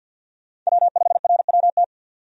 Morse code, heard as a keyed tone, about 50 wpm, 700 Hz. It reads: W5RWT
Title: USA Call Signs - 5 Characters - 50wpm